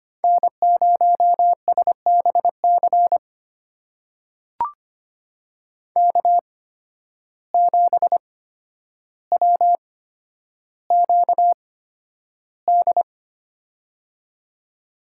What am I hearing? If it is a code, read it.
Morse code, heard as a keyed tone, about 25 wpm, spaced as for 11 wpm, 700 Hz. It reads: N0SBC K7WQD
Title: USA Call Signs - 8x Character Spacing - 25wpm